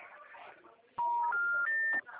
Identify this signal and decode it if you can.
Special information tone, with speech.